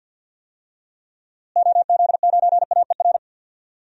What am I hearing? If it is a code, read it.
Morse code, heard as a keyed tone, about 50 wpm, 700 Hz. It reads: OZ9AER
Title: DXCC-100 Call Signs - 6 Characters - Repeated 2x - 50wpm